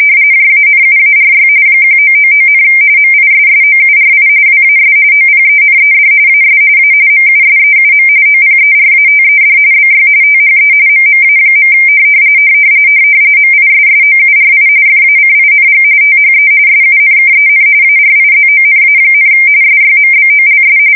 RTTY, about 45.45 baud. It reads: JANUARY 1999, MISSOURI WAS OPENED AS A MUSEUM OPERATED BY THE MMA. MISSOURI FACING THE SUNKEN ARIZONA, SYMBOLS
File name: fsk170 / USS-Missouri